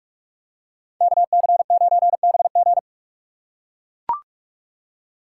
Morse code, heard as a keyed tone, about 42 wpm, 700 Hz. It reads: KC9BG
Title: K1USN SST QSO Practice - 45wpm